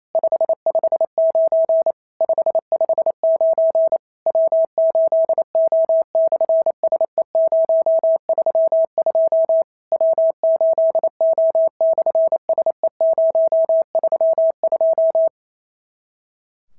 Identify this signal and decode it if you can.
Morse code, heard as a keyed tone, about 28 wpm, 650 Hz. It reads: 559 559 W8O/SE032 W8O/SE032
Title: SOTA QSO Practice 28wpm